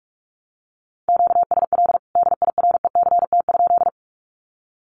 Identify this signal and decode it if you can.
Morse code, heard as a keyed tone, about 44 wpm, 700 Hz. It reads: QSL DIRECT?
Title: Common QSO Phrases 45wpm